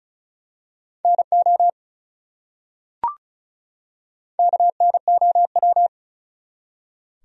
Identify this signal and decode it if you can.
Morse code, heard as a keyed tone, about 34 wpm, 700 Hz. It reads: NO KNOW
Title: Top 100 Words 35wpm